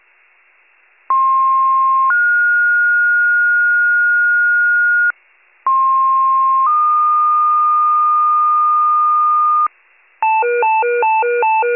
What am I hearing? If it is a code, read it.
Two-tone page, heard as a one-second tone, then a three-second tone, about 1060 then 1430 Hz, about 1060 then 1220 Hz, with noise. It is heard from 1.1 to 9.7 s.
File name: pager / 98 FIRE / wcalert